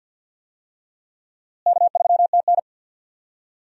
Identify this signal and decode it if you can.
Morse code, heard as a keyed tone, about 52 wpm, 700 Hz. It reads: K3TN
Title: USA Call Signs - 4 Characters - Repeated 3x - 50wpm